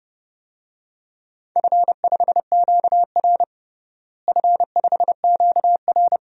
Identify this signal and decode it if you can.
Morse code, heard as a keyed tone, about 30 wpm, 700 Hz. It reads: F5QR F5QR
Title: DXCC-100 Call Signs - 4 Characters - Repeated 2x - 30wpm